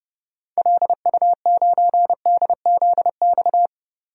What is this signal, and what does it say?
Morse code, heard as a keyed tone, about 30 wpm, 700 Hz. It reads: LU9DZX
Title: DXCC-100 Call Signs - 6 Characters - 30wpm